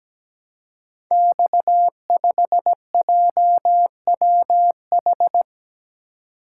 Morse code, heard as a keyed tone, about 17 wpm, 700 Hz. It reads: X5JWH
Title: Code Groups - 5 Letter:Numbers - 17wpm